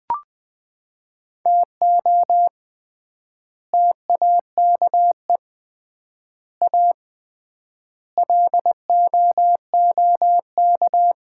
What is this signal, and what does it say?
Morse code, heard as a keyed tone, about 20 wpm, 700 Hz. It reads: TO TAKE A LOOK
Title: 3x Word-Spacing - Sets of 5 Words - 20wpm